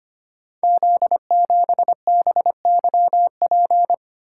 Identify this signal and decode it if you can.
Morse code, heard as a keyed tone, about 25 wpm, 700 Hz. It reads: Z7BYP
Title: Code Groups - 5 Letter:Numbers - 25wpm